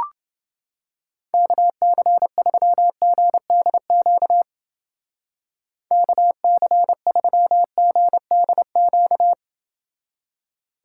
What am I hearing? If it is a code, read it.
Morse code, heard as a keyed tone, about 29 wpm, 700 Hz. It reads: KC3GDQ KC3GDQ